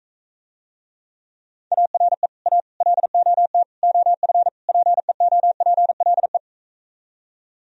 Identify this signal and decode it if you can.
Morse code, heard as a keyed tone, about 45 wpm, 700 Hz. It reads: ARE A LOT OF PEOPLE